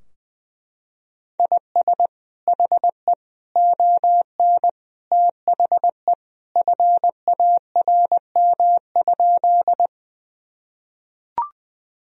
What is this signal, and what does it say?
Morse code, heard as a keyed tone, about 20 wpm, 700 Hz. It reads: IS HE ON THE FARM?
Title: Sentences from Top 1000 Words - Review Part 2 - 20wpm